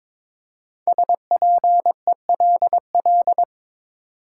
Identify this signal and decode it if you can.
Morse code, heard as a keyed tone, about 22 wpm, 700 Hz. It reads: SPELL
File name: practice / 22wpm / 5 Letter Words - First 3 Letters Given - 22wpm